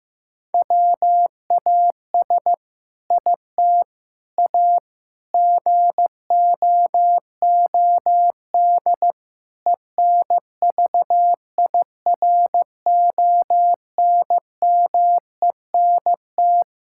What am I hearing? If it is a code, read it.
Morse code, heard as a keyed tone, about 16 wpm, 700 Hz. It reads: WAS IT A GOOD ENVIRONMENT
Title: Sentences from Top 1000 Words - Review Part 5 - 15wpm